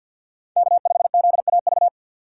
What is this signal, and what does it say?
Morse code, heard as a keyed tone, about 53 wpm, 700 Hz. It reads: K5ZAV
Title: USA Call Signs - 5 Characters - 3x Character Spacing - 50wpm